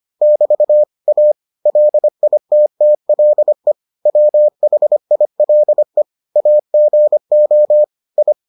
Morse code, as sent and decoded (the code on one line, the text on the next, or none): -...- / .- / .-.. .. - - .-.. / .-- .... .. .-.. / .- --. --- / ..
<BT> A LITTLE WHILE AGO I